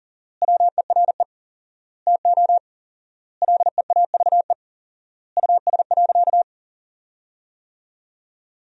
Morse code, heard as a keyed tone, about 40 wpm, 700 Hz. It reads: WERE TO LEAVE US.
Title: 4x Word-Spacing - Sentences from Top 400 Words - 40wpm